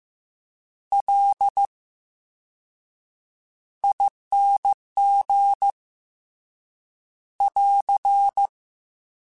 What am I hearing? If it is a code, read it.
Morse code, heard as a keyed tone, about 15 wpm, spaced as for 7 wpm, 800 Hz. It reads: LING <AR>
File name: Special INGs-mp3s / Special INGs-mp3s / VerySlowToFaster Ing @13